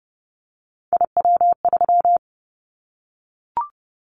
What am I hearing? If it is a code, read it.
Morse code, heard as a keyed tone, about 30 wpm, 700 Hz. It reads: IW3
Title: Callsign Prefixes for Top 25 DXCC - 30wpm